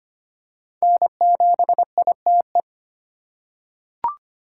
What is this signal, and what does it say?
Morse code, heard as a keyed tone, about 25 wpm, 700 Hz. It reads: N7ITE
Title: USA Call Signs - 5 Characters - 2x Character Spacing - 25wpm